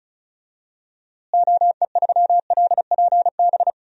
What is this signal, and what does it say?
Morse code, heard as a keyed tone, about 36 wpm, 700 Hz. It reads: OE3LPB
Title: DXCC-100 Call Signs - 6 Characters - Repeated 2x - 35wpm